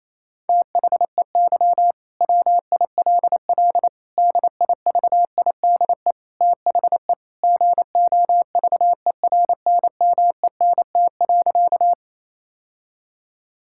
Morse code, heard as a keyed tone, about 28 wpm, 700 Hz. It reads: THEY WILL DIVIDE THE GOVERNMENT.
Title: Sentences from Top 1600 Words 28wpm